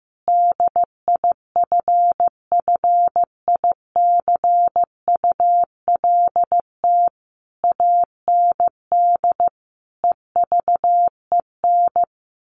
Morse code, heard as a keyed tone, about 15 wpm, 700 Hz. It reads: DIFFICULT AND EVEN
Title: Sentences from Top 1200 Words 15wpm